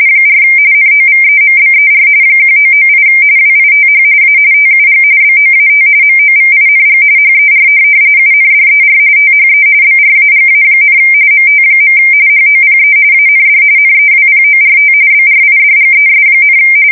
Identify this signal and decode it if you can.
RTTY, about 45.45 baud. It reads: 4 NOVEMBER 1998 WITH A TUG BOAT MOVING HER UNDER THE BRAGA, MT. HOPE AND NEWPORT BRIDGES,
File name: fsk170 / USS-Massachusetts